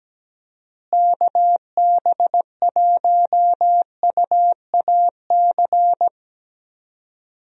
Morse code, heard as a keyed tone, about 17 wpm, 700 Hz. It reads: KB1UAC
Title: USA Call Signs - 6 Characters - Repeated 3x - 17wpm